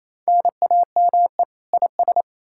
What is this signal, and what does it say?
Morse code, heard as a keyed tone, about 38 wpm, spaced as for 33 wpm, 700 Hz. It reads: NAME IS
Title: Speed-Racing Common QSO Phrases - 28wpm